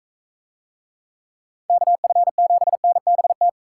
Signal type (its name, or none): Morse code